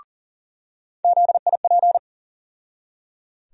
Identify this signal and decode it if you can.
Morse code, heard as a keyed tone, about 40 wpm, 700 Hz. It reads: ZIP
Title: USA Call Sign Postfixes - 3 Letters 40wpm